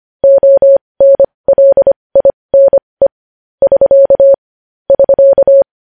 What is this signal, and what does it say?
Morse code, heard as a keyed tone, about 25 wpm, spaced as for 20 wpm, 550 Hz. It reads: ONLINE <SK> <SK>
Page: news-headlines-in-morse-code-at-20-wpm